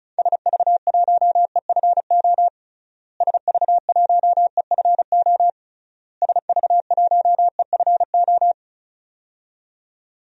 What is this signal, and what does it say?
Morse code, heard as a keyed tone, about 36 wpm, 700 Hz. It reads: SV1EFO SV1EFO SV1EFO
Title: DXCC-100 Call Signs - 6 Characters - Repeated 3x - 35wpm